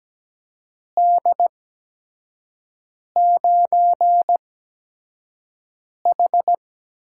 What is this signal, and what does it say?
Morse code, heard as a keyed tone, about 17 wpm, spaced as for 5 wpm, 700 Hz. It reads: D9H
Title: USA Call Signs - 4 Characters - 8x Character Spacing - 17wpm